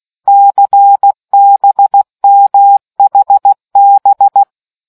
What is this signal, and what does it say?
Morse code, heard as a keyed tone, about 16 wpm, 800 Hz. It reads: CBMHB